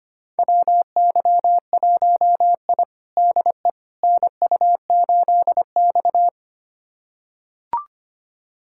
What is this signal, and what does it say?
Morse code, heard as a keyed tone, about 25 wpm, 700 Hz. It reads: WY1I DE NU8X
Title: USA Callsign DE Callsign - 4 Characters - 25wpm